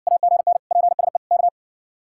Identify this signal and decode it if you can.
Morse code, heard as a keyed tone, about 55 wpm, 700 Hz. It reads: AGN PSE K